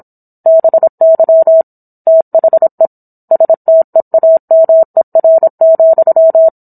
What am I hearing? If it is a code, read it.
Morse code, heard as a keyed tone, about 27 wpm, 650 Hz. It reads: BY THE STEAMER,